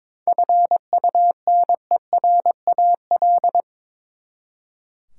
Morse code, heard as a keyed tone, about 22 wpm, 700 Hz. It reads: FUNERAL
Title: Seven Letter Words - 22wpm